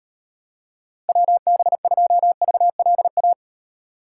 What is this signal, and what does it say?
Morse code, heard as a keyed tone, about 36 wpm, 700 Hz. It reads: WB2VLA